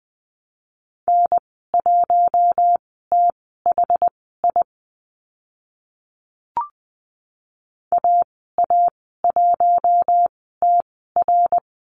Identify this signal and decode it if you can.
Morse code, heard as a keyed tone, about 20 wpm, spaced as for 14 wpm, 700 Hz. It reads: N1THI AA1TR
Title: 062 Lesson 11g - Letter h - Call Signs 20wpm